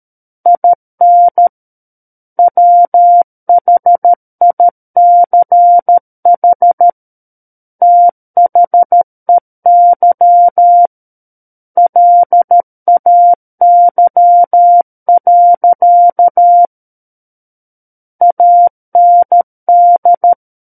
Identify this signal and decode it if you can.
Morse code, heard as a keyed tone, about 13 wpm, 700 Hz. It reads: IN WHICH THEY LAY. AND
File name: Book One / 04-Chapter 4 - 13 WPM